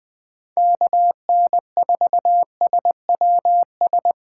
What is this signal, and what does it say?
Morse code, heard as a keyed tone, about 20 wpm, 700 Hz. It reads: KN4SWS
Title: 2023 POTA Activator Call Signs - 20wpm